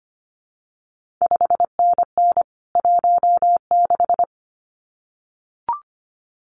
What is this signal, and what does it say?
Morse code, heard as a keyed tone, about 25 wpm, 700 Hz. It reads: 5NN 16
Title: CQ WW DX Contest Exchange - 25wpm